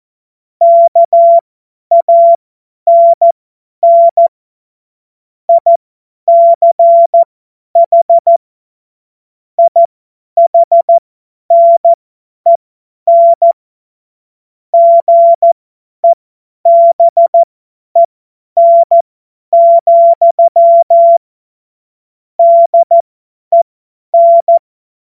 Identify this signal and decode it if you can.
Morse code, heard as a keyed tone, about 14 wpm, spaced as for 10 wpm, 680 Hz. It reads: KANN ICH IHNEN GEBEN, DEN
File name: Morsekurs / Witze / Witz 30 Januar 2024 50 F70